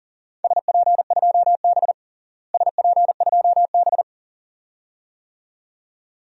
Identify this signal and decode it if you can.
Morse code, heard as a keyed tone, about 40 wpm, 700 Hz. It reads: SP2B SP2B
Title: DXCC-100 Call Signs - 4 Characters - Repeated 2x - 40wpm